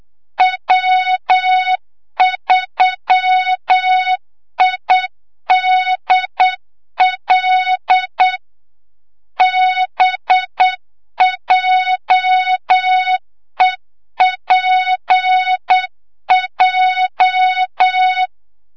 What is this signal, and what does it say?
Morse code, heard as a keyed tone, about 8 wpm, 3700 Hz, with noise. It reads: W3IDL BJEPJ
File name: Morze-HA2MN / hang / 25 GYAKORLAT